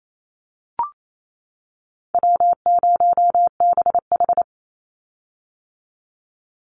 Morse code, heard as a keyed tone, about 28 wpm, 700 Hz. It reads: W0BH
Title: Easy Call Signs - 28wpm